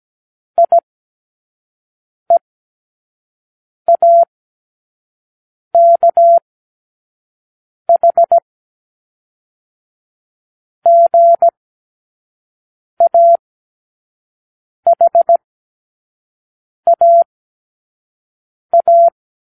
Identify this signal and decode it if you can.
Morse code, heard as a keyed tone, about 17 wpm, spaced as for 5 wpm, 680 Hz. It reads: IEAKH GAHAA
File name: k7qo / Lesson18(A-L)